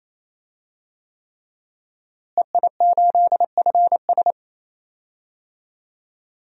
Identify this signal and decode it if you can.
Morse code, heard as a keyed tone, about 28 wpm, 700 Hz. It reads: EI8FS
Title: DXCC-100 Call Signs - 5 Characters - Repeated 3x - 28wpm